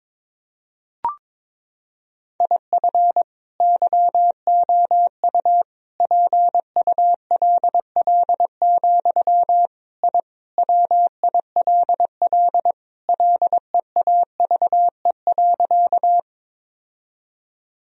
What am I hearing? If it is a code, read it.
Morse code, heard as a keyed tone, about 21 wpm, 700 Hz. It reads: IF YOU PULL, I WILL LEAVE.
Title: Sentences from Top 1000 Words - Review Part 2 - 22wpm